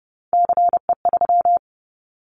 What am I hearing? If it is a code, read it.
Morse code, heard as a keyed tone, about 30 wpm, 700 Hz. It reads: CE3
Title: Callsign Prefixes for Top 100 DXCC - 30wpm